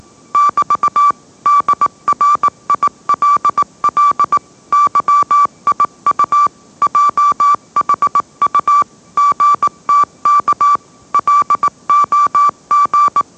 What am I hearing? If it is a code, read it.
Morse code, heard as a keyed tone, about 18 wpm, 1160 Hz. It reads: <BT> DRILL YIU JHU GTK LOG
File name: mrx / mrx082 20wpm